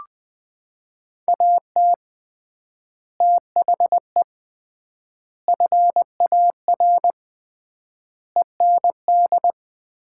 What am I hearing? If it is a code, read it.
Morse code, heard as a keyed tone, about 20 wpm, 700 Hz. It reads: AT THE FAR END
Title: 3x Word-Spacing - Sets of 5 Words - 20wpm